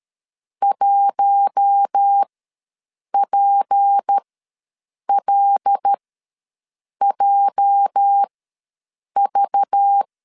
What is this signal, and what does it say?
Morse code, heard as a keyed tone, about 13 wpm, spaced as for 7 wpm, 800 Hz. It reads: 1PLJV
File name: LEZIONI / 8 YJ45 / 8 YJ ws 12wpm 3342